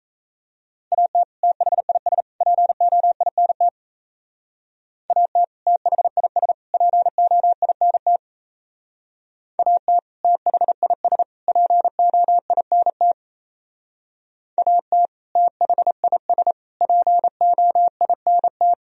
Morse code, heard as a keyed tone, about 33 wpm, 700 Hz. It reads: AT THIS POINT AT THIS POINT AT THIS POINT AT THIS POINT
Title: Speed-Racing - Sets of 3 Words 28wpm